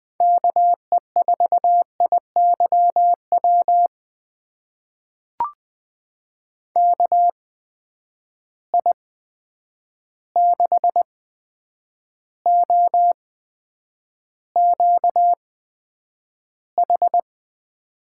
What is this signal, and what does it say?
Morse code, heard as a keyed tone, about 20 wpm, spaced as for 9 wpm, 700 Hz. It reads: KE4IYW KI6OQH
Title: USA Call Signs - 6 Characters - 8x Character Spacing - 20wpm